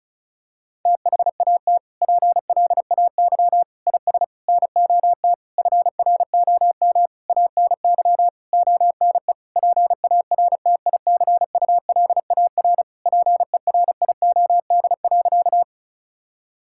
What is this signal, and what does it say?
Morse code, heard as a keyed tone, about 34 wpm, 700 Hz. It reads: THAT PLAY IS NOT FROM ANY ONE PARTICULAR PERIOD.
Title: Sentences from Top 1000 Words - Review Part 5 - 35wpm